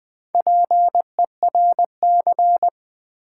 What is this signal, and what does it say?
Morse code, heard as a keyed tone, about 20 wpm, 700 Hz. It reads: PERC
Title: Build A Word - 4 Letters At A Time - 10-Letter Words - 20wpm